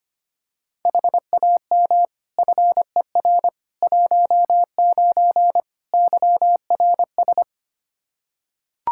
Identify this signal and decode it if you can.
Morse code, heard as a keyed tone, about 25 wpm, 700 Hz. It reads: HAM FER 19 YRS
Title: Practice QSO - Exchange 3 - More Info - 25wpm